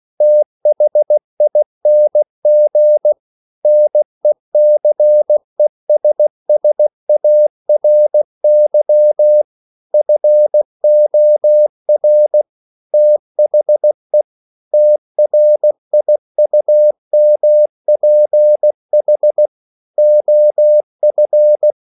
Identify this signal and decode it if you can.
Morse code, heard as a keyed tone, about 16 wpm, 600 Hz. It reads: THING NECESSARY FOR THE TRIUMPH OF
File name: CW / Practice / EnglishQuotations0416